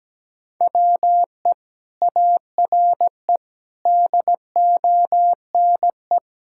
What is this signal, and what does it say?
Morse code, heard as a keyed tone, about 17 wpm, 700 Hz. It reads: WE ARE DONE
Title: Sentences from Top 1600 Words 17wpm